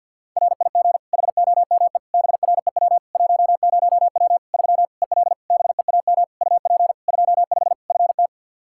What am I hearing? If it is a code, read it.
Morse code, heard as a keyed tone, about 48 wpm, 700 Hz. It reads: RIG HOME BREW 10W 3 EL BEAM UP 25 FT
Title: K1USN SST QSO Practice 50wpm